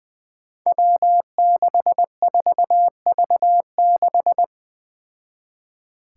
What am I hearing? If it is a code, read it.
Morse code, heard as a keyed tone, about 20 wpm, 700 Hz. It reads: W64V6
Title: Five Challenging Letter-Number Combinations 20wpm